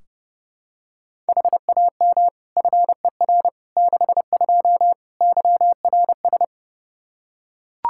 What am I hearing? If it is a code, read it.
Morse code, heard as a keyed tone, about 30 wpm, 700 Hz. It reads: HAM FER 62 YRS
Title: K1USN SST QSO Practice 30wpm